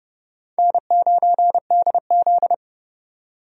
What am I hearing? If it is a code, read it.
Morse code, heard as a keyed tone, about 32 wpm, 700 Hz. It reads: N9DZ
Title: USA Call Signs - 4 Characters - 30wpm